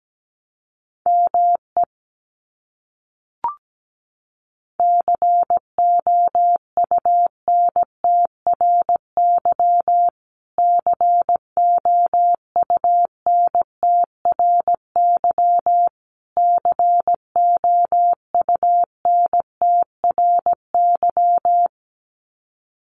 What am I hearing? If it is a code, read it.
Morse code, heard as a keyed tone, about 17 wpm, 700 Hz. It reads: ME COUNTRY COUNTRY COUNTRY
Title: Top 300 Words Repeated 17wpm